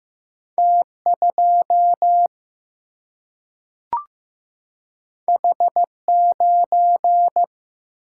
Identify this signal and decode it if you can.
Morse code, heard as a keyed tone, about 15 wpm, 700 Hz. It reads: T2 H9
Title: Single-Letter Single-Number 15wpm